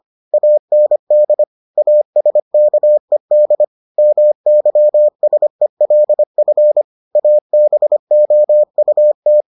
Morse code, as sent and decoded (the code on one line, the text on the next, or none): .- -. -.. / .- ... -.- . -.. / -- -.-- ... .-.. ..-. / .- -... --- ..- -
AND ASKED MYSELF ABOUT